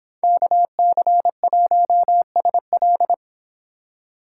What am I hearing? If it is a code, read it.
Morse code, heard as a keyed tone, about 24 wpm, 700 Hz. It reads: KC1SL